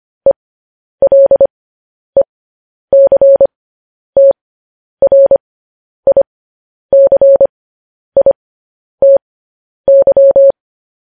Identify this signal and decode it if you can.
Morse code, heard as a keyed tone, about 25 wpm, spaced as for 10 wpm, 550 Hz. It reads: ELECTRICITY
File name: podcast / News headlines-1x-25x10